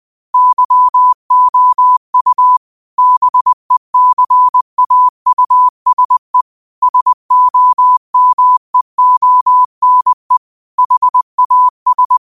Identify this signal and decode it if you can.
Morse code, heard as a keyed tone, about 20 wpm, 1000 Hz. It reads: YOU BECAUSE SOMEONE HAS
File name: qotdfiles / 20wpm